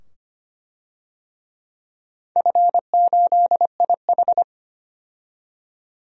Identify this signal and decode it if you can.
Morse code, heard as a keyed tone, about 25 wpm, 700 Hz. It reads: F8IH
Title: DXCC-100 Call Signs - 4 Characters - 25wpm